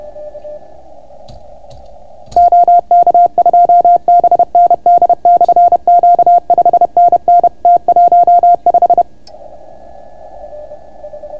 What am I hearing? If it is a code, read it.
Morse code, heard as a keyed tone, about 30 wpm, 690 Hz. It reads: OK2BND/Q 5NN T15